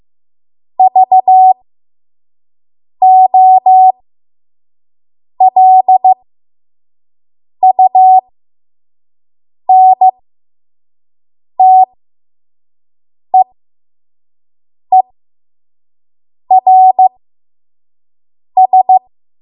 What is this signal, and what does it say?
Morse code, heard as a keyed tone, about 15 wpm, spaced as for 5 wpm, 750 Hz. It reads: VOLUNTEERS
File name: Morse / 05WPM